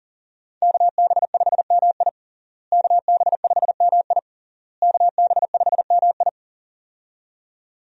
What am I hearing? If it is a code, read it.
Morse code, heard as a keyed tone, about 40 wpm, 700 Hz. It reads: KB5MI KB5MI KB5MI
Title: USA Call Signs - 5 Characters - Repeated 3x - 40wpm